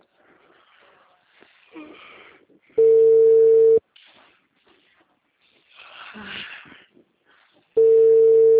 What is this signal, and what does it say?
Ringback tone (China, 450 Hz).
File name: out- -2032- - - .73